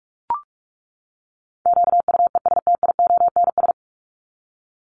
Morse code, heard as a keyed tone, about 44 wpm, 700 Hz. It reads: QUESTIONS
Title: ES:S Suffix - Top 100 Words with Base Words 45wpm